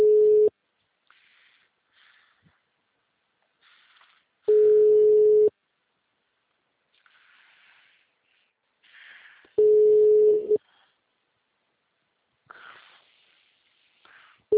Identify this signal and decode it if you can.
Ringback tone (Europe, 425 Hz).